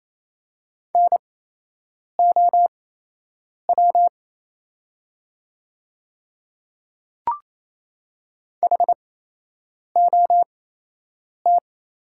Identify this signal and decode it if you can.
Morse code, heard as a keyed tone, about 28 wpm, spaced as for 8 wpm, 700 Hz. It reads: NOW HOT
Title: 8x Character Spacing - Top 500 Words - 28wpm